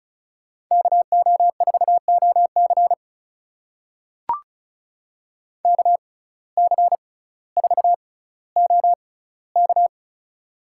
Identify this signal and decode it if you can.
Morse code, heard as a keyed tone, about 36 wpm, spaced as for 18 wpm, 700 Hz. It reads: KO4OC KC4OK
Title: USA Call Signs - 5 Characters - 6x Character Spacing - 35wpm